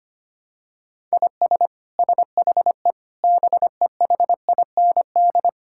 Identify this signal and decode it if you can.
Morse code, heard as a keyed tone, about 25 wpm, 700 Hz. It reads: IS SHE BEHIND
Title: Sentences from Top 1000 Words - Review Part 6 - 25wpm